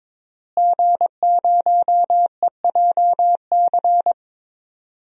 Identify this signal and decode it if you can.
Morse code, heard as a keyed tone, about 22 wpm, 700 Hz. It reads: G0EJC
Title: DXCC-100 Call Signs - 5 Characters - Repeated 3x - 22wpm